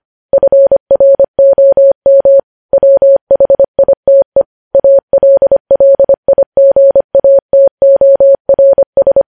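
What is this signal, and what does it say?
Morse code, heard as a keyed tone, about 25 wpm, 550 Hz. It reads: FROM WHITE ALLIGATORS
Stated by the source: Morse code news headlines sent at 25 words per minute.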